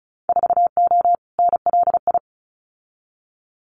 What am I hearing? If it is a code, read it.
Morse code, heard as a keyed tone, about 34 wpm, 700 Hz. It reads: 4O NLI
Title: Winter Field Day Exchanges Encore 35wpm